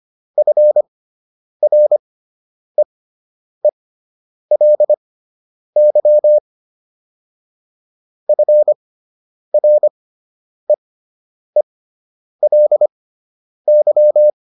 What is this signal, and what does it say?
Morse code, heard as a keyed tone, about 25 wpm, spaced as for 9 wpm, 600 Hz. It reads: FREELY FREELY